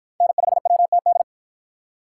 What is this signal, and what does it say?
Morse code, heard as a keyed tone, about 53 wpm, 700 Hz. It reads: N5WTD